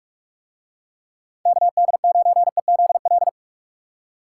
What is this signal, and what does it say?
Morse code, heard as a keyed tone, about 41 wpm, 700 Hz. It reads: KD9EZL